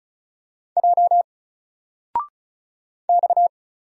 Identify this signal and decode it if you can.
Morse code, heard as a keyed tone, about 35 wpm, 700 Hz. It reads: J X
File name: Single-Letter-Number-35wpm